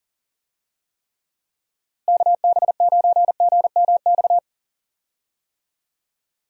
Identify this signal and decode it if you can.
Morse code, heard as a keyed tone, about 40 wpm, 700 Hz. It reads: KB9GMX
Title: USA Call Signs - 6 Characters - Repeated 3x - 40wpm